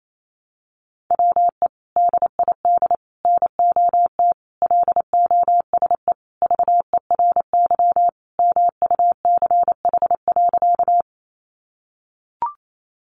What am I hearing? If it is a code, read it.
Morse code, heard as a keyed tone, about 28 wpm, 700 Hz. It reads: WE DID NOT LOSE VERY MUCH.
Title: Sentences from Top 1100 Words 28wpm